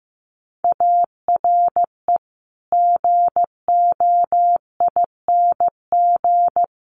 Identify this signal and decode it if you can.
Morse code, heard as a keyed tone, about 15 wpm, 700 Hz. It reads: ARE GOING
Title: Sets of 4 Words 15wpm